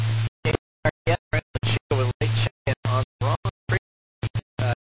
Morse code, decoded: none

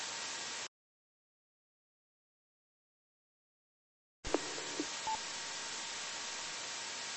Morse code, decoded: none

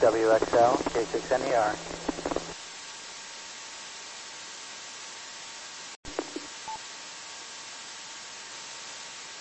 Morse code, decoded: none

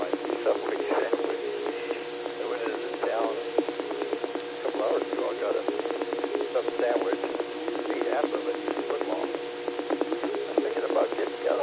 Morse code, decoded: none

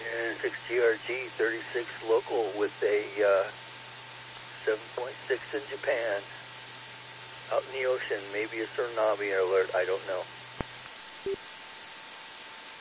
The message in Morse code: none

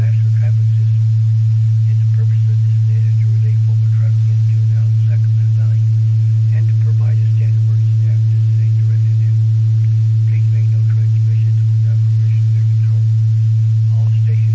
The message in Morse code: none

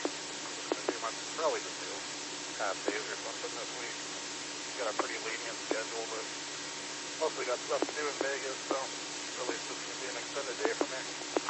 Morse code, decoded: none

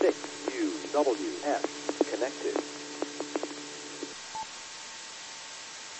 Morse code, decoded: none

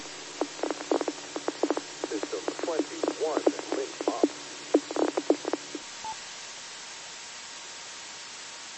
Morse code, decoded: none

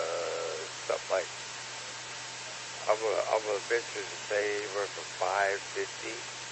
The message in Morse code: none

